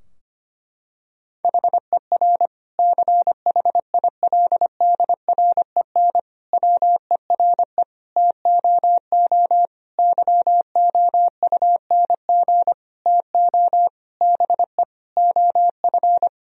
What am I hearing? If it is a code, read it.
Morse code, heard as a keyed tone, about 25 wpm, 700 Hz. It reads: HER CHILDREN WERE TOO YOUNG TO BE OF
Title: Sentences from Top 1000 Words - Review Part 4 - 25wpm